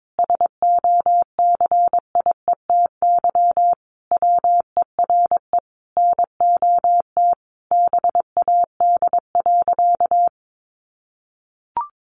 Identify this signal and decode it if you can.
Morse code, heard as a keyed tone, about 22 wpm, 700 Hz. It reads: SOCIETY WERE NOT BAD.
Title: Sentences from Top 900 Words 22wpm